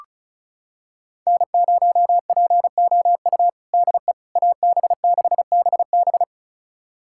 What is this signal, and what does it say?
Morse code, heard as a keyed tone, about 36 wpm, 700 Hz. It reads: N0POU DE AB6BB
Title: USA Callsign DE Callsign - 5 Characters - 35wpm